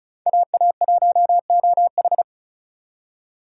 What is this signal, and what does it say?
Morse code, heard as a keyed tone, about 34 wpm, 700 Hz. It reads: AA1OH
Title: USA Call Signs - 5 Characters - 2x Character Spacing - 35wpm